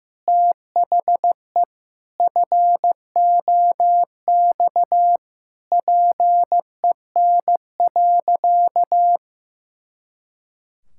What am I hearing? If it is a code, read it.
Morse code, heard as a keyed tone, about 15 wpm, 700 Hz. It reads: THE FOX PEN.
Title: Pangram Challenge v2 - 15wpm